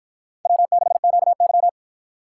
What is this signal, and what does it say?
Morse code, heard as a keyed tone, about 50 wpm, 700 Hz. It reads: W6QY